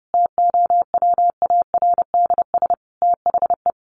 Morse code, heard as a keyed tone, about 30 wpm, 700 Hz. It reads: TOWARDS THE
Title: Additional Easy Phrases 30wpm